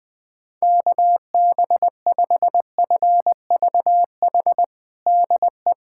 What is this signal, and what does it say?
Morse code, heard as a keyed tone, about 20 wpm, 700 Hz. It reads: KB5FVH DE
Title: USA Callsign DE Callsign 20wpm